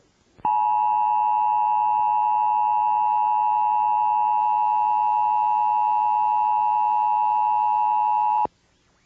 Emergency Alert System (SAME) tones (the 853 and 960 Hz attention tone). It is heard from 0.4 to 8.5 s.